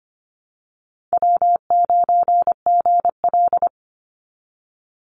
Morse code, heard as a keyed tone, about 25 wpm, 700 Hz. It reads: W9GL
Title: Call Signs Repeated 2x - 25wpm